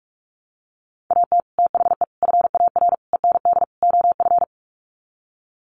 Morse code, heard as a keyed tone, about 44 wpm, 700 Hz. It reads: AT THE FAR END OF
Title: Sets of 5 Words 45wpm